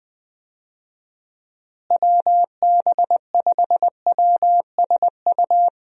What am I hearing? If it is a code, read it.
Morse code, heard as a keyed tone, about 20 wpm, 700 Hz. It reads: WB5WSU
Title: USA Call Signs - 6 Characters - Repeated 2x - 20wpm